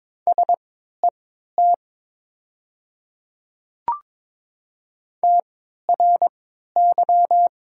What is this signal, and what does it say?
Morse code, heard as a keyed tone, about 22 wpm, spaced as for 13 wpm, 700 Hz. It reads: SET TRY
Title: 3x Character Spacing - Top 300 Words - 22wpm